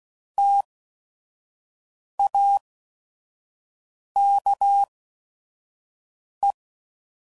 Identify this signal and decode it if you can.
Morse code, heard as a keyed tone, about 16 wpm, spaced as for 5 wpm, 800 Hz. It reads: TAKE